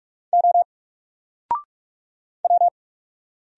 Morse code, heard as a keyed tone, about 45 wpm, 700 Hz. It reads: O W
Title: Single Letter-Number 45wpm